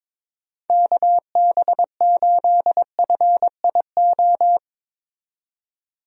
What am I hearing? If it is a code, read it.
Morse code, heard as a keyed tone, about 22 wpm, 700 Hz. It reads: KB8FIO
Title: USA Call Signs - 6 Characters - 22wpm